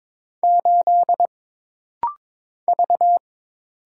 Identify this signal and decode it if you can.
Morse code, heard as a keyed tone, about 22 wpm, 700 Hz. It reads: 8 V
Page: This rapid-fire timing is geared to push your ICR (Instant Character Recognition) skills.